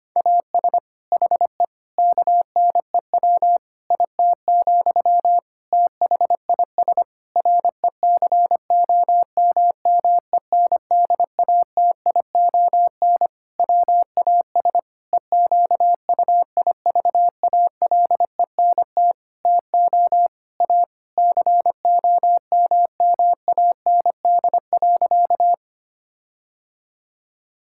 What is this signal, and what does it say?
Morse code, heard as a keyed tone, about 25 wpm, 700 Hz. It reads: AS HE KNEW IT, THIS RECOMMENDATION WAS EQUIVALENT TO A COMMAND.
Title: Sentences from Top 1600 Words 25wpm